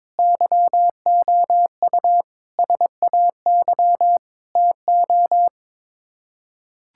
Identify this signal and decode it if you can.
Morse code, heard as a keyed tone, about 22 wpm, 700 Hz. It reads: YOU SAY TO
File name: practice / 22wpm / Build A Phrase - 5-Word Phrases - 22wpm